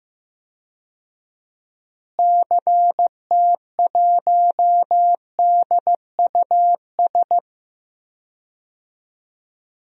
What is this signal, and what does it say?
Morse code, heard as a keyed tone, about 15 wpm, 700 Hz. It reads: CT1DUS
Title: DXCC-100 Call Signs - 6 Characters - Repeated 2x - 15wpm